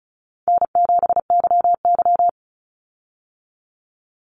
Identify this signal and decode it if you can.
Morse code, heard as a keyed tone, about 34 wpm, 700 Hz. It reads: N7YY
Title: Call Signs Repeated 2x - 35wpm